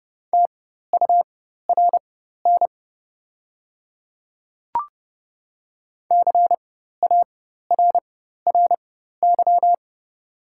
Morse code, heard as a keyed tone, about 30 wpm, spaced as for 14 wpm, 700 Hz. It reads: TURN CARRY
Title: 4x Character Spacing - Top 300 Words - 30wpm